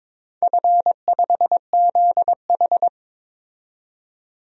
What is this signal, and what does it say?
Morse code, heard as a keyed tone, about 22 wpm, 690 Hz. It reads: F5ZH
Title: DXCC-100 Call Signs - 4 Characters - 22wpm